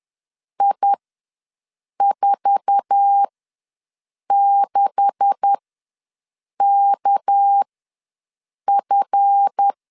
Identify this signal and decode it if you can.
Morse code, heard as a keyed tone, about 10 wpm, spaced as for 6 wpm, 800 Hz. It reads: I46KF
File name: LEZIONI / 9 286 ws 10wpm 4032